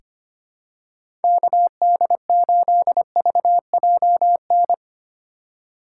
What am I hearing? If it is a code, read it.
Morse code, heard as a keyed tone, about 25 wpm, 700 Hz. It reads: KD8VJN
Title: USA Call Signs - 6 Characters - 3x Character Spacing - 25wpm